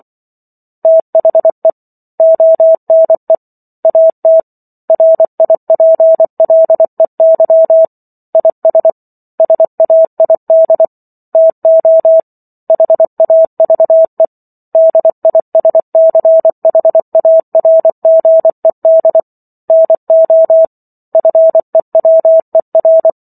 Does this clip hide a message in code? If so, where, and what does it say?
Yes: Morse code from 0.8 to 23.1 s, heard as a keyed tone, about 24 wpm, 650 Hz. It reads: THE ONE AT RIPLEY IS SAID TO HAVE DISCHARGED NO FEWER